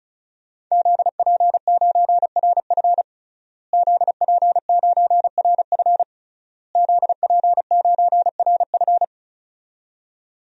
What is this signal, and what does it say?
Morse code, heard as a keyed tone, about 34 wpm, 700 Hz. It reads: ZP9RF ZP9RF ZP9RF
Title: DXCC-100 Call Signs - 5 Characters - Repeated 3x - 35wpm